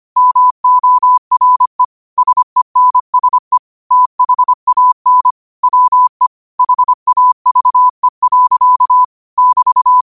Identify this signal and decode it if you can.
Morse code, heard as a keyed tone, about 25 wpm, 1000 Hz. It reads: MORE SENSE THAN WE HAVE. <BT>
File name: qotdfiles / 25wpm